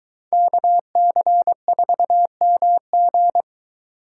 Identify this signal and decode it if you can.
Morse code, heard as a keyed tone, about 23 wpm, 700 Hz. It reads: KC4MG